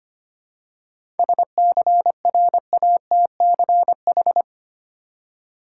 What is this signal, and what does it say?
Morse code, heard as a keyed tone, about 25 wpm, 700 Hz. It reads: SCRATCH
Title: Seven Letter Words - 25wpm